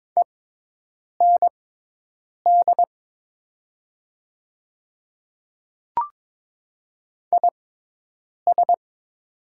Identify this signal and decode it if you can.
Morse code, heard as a keyed tone, about 22 wpm, spaced as for 8 wpm, 700 Hz. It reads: END IS
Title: 6x Character Spacing - Top 200 Words - 22wpm